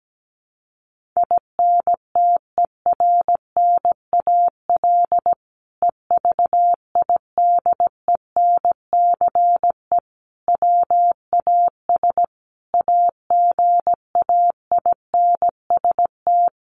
Morse code, heard as a keyed tone, about 17 wpm, 700 Hz. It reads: INTERNAL EVIDENCE WAS AGAINST
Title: Sentences from Top 1400 Words 17wpm